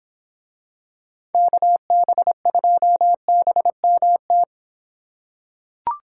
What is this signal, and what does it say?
Morse code, heard as a keyed tone, about 25 wpm, 700 Hz. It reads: KB2BMT